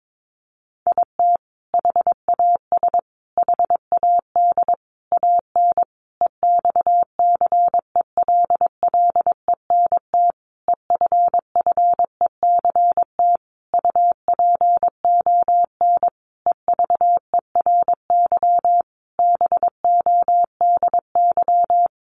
Morse code, heard as a keyed tone, about 22 wpm, 700 Hz. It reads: IT HAS HAD AN EXCELLENT EFFECT UPON EVERY BODY
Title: Sentences from Top 1100 Words 22wpm